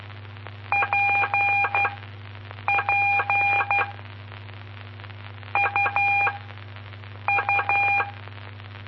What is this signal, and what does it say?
Morse code, heard as a keyed tone, about 12 wpm, spaced as for 7 wpm, 2400 Hz. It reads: PP UU
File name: pp5bar / aulas de telegrafia / Aula-06